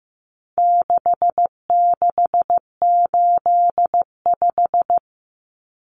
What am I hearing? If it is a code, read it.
Morse code, heard as a keyed tone, about 15 wpm, 700 Hz. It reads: 6685
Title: Common Cut Numbers - 4-Digit 15wpm